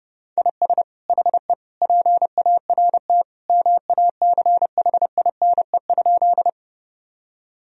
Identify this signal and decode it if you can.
Morse code, heard as a keyed tone, about 30 wpm, 700 Hz. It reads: IS HE PART MACHINE?
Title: Sentences from Top 1000 Words - Review Part 2 - 30wpm